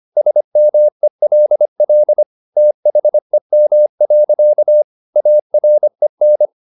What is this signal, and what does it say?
Morse code, heard as a keyed tone, about 25 wpm, 600 Hz. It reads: SMELL THEM. AREN
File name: the old man and the sea part 1 25wpm